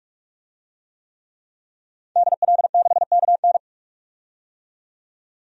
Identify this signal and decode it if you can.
Morse code, heard as a keyed tone, about 45 wpm, 700 Hz. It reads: DL6KN
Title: DXCC-100 Call Signs - 5 Characters - 45wpm